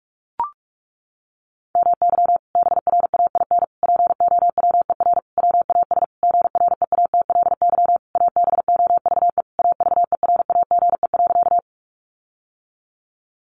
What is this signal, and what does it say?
Morse code, heard as a keyed tone, about 44 wpm, 700 Hz. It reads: MY BRAIN POWER WAS GREATLY ABOVE AVERAGE.
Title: Sentences from Top 1400 Words 45wpm